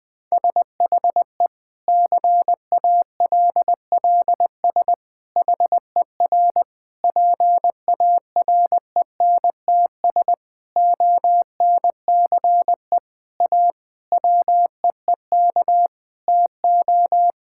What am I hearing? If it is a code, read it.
Morse code, heard as a keyed tone, about 20 wpm, 700 Hz. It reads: SHE CALLS HER PARENTS ONCE A WEEK TO
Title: Difficult Sentences with Hints - 20wpm